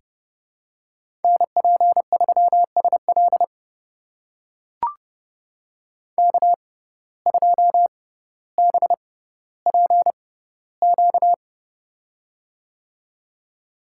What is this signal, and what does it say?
Morse code, heard as a keyed tone, about 30 wpm, spaced as for 15 wpm, 700 Hz. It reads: NP3SL K2BPQ
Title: USA Call Signs - 5 Characters - 6x Character Spacing - 30wpm